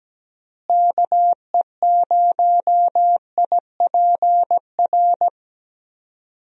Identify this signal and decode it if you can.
Morse code, heard as a keyed tone, about 17 wpm, 700 Hz. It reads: KE0IPR
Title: USA Call Signs - 6 Characters - Repeated 3x - 17wpm